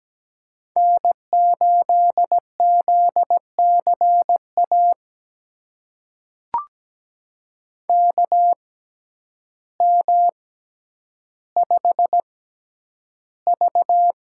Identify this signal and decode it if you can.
Morse code, heard as a keyed tone, about 17 wpm, spaced as for 9 wpm, 700 Hz. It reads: N8ZCA KM5V
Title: USA Call Signs - 5 Characters - 6x Character Spacing - 17wpm